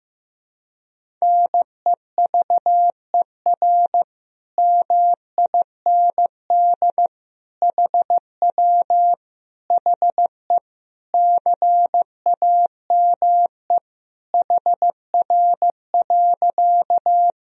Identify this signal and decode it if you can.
Morse code, heard as a keyed tone, about 15 wpm, 700 Hz. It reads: NEVER MIND HW HE CAME HR.
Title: Sentences from Top 500 Words and Abbreviations 15wpm